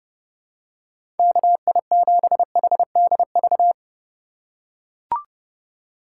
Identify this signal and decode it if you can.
Morse code, heard as a keyed tone, about 30 wpm, 700 Hz. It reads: KI7HDV
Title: USA Call Signs - 6 Characters - 2x Character Spacing - 30wpm